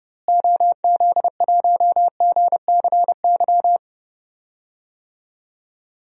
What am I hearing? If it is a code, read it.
Morse code, heard as a keyed tone, about 30 wpm, 700 Hz. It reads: OZ1GCY
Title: DXCC-100 Call Signs - 6 Characters - Repeated 3x - 30wpm